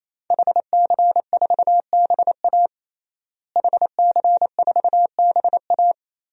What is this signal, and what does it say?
Morse code, heard as a keyed tone, about 28 wpm, 700 Hz. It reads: HC4BA HC4BA
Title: DXCC-100 Call Signs - 5 Characters - Repeated 3x - 28wpm